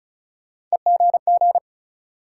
Morse code, heard as a keyed tone, about 36 wpm, 700 Hz. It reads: EGG
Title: Sentences from Top 1000 Words - Review Part 7 - New Word Drill - 35wpm